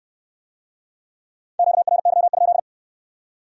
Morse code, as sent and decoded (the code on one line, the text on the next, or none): --.- .-. -.-- ..--..
QRY?